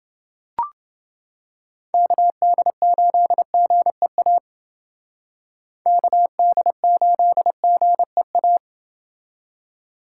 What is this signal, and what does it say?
Morse code, heard as a keyed tone, about 29 wpm, 700 Hz. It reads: KD8GEA KD8GEA